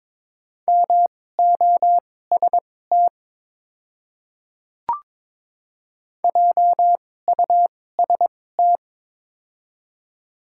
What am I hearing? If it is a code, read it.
Morse code, heard as a keyed tone, about 22 wpm, spaced as for 16 wpm, 700 Hz. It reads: MOST JUST
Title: 2x Character Spacing - Top 200 Words - 22wpm